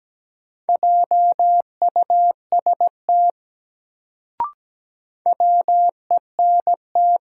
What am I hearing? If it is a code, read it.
Morse code, heard as a keyed tone, about 17 wpm, 700 Hz. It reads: JUST WENT
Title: Sentences from Top 200 Words - New Word Drill - 17wpm